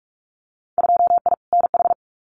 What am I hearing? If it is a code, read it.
Morse code, heard as a keyed tone, about 46 wpm, 700 Hz. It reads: 2I NH
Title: Winter Field Day Exchanges Encore 45wpm